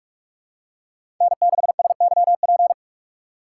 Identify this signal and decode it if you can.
Morse code, heard as a keyed tone, about 42 wpm, 700 Hz. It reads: N6SYP
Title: USA Call Signs - 5 Characters - 4x Character Spacing - 45wpm